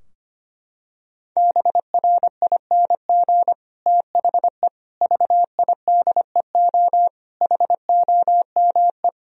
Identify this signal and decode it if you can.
Morse code, heard as a keyed tone, about 25 wpm, 700 Hz. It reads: BRING THE VIDEO HOME
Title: Sentences from Top 1000 Words - Review Part 4 - 25wpm